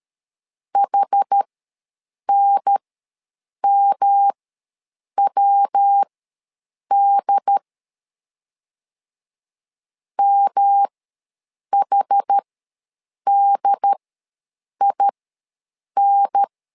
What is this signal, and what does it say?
Morse code, heard as a keyed tone, about 13 wpm, spaced as for 7 wpm, 800 Hz. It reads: HNMWD MHDIN
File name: LEZIONI / 3 MWRO / 3 MWRO ws 12wpm 1914